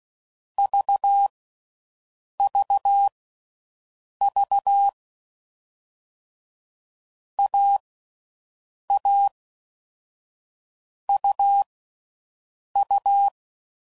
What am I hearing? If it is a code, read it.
Morse code, heard as a keyed tone, about 16 wpm, spaced as for 6 wpm, 800 Hz. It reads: VVV AA UU
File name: Combinations / Difficult Combos / AUV ShortRun